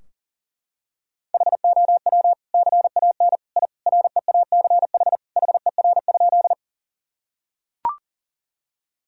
Morse code, heard as a keyed tone, about 40 wpm, 700 Hz. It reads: HOW CAN I REACH HER?
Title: Sentences from Top 1000 Words - Review Part 2 - 40wpm